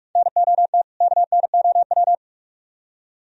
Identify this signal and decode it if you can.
Morse code, heard as a keyed tone, about 42 wpm, 700 Hz. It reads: NOT KNOW